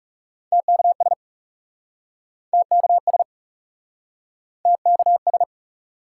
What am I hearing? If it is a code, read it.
Morse code, heard as a keyed tone, about 40 wpm, 700 Hz. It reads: TKS TKS TKS